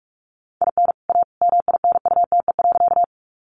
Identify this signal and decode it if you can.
Morse code, heard as a keyed tone, about 44 wpm, 700 Hz. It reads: IN A MINUTE.
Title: Sentences from Top 500 Words 45wpm